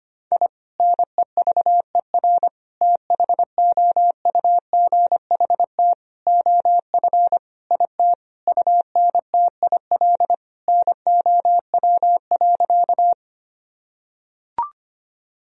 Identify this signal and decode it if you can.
Morse code, heard as a keyed tone, about 25 wpm, 700 Hz. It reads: I NEVER THOUGHT OF IT UNTIL NOW.
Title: Sentences from Top 1000 Words - Review Part 1 - 25wpm